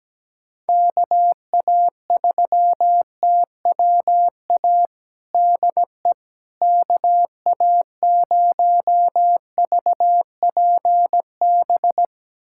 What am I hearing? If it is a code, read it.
Morse code, heard as a keyed tone, about 17 wpm, 700 Hz. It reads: KA3TWA DE KA0VPB
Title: USA Callsign DE Callsign - 6 Characters - 17wpm